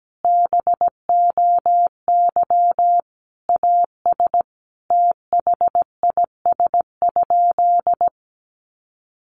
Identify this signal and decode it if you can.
Morse code, heard as a keyed tone, about 17 wpm, 700 Hz. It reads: BOY AS THIS?
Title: Sentences from Top 400 Words 17wpm